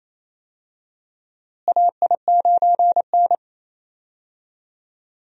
Morse code, heard as a keyed tone, about 28 wpm, 700 Hz. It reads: AI9N
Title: USA Call Signs - 4 Characters - Repeated 3x - 28wpm